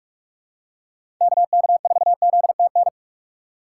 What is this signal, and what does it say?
Morse code, heard as a keyed tone, about 46 wpm, 690 Hz. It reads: KK4ZTN
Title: USA Call Signs - 4x Character Spacing - 45wpm